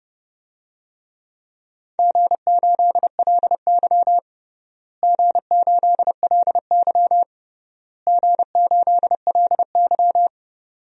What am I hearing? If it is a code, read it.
Morse code, heard as a keyed tone, about 30 wpm, 700 Hz. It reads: G8LY G8LY G8LY
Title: DXCC-100 Call Signs - 4 Characters - Repeated 3x - 30wpm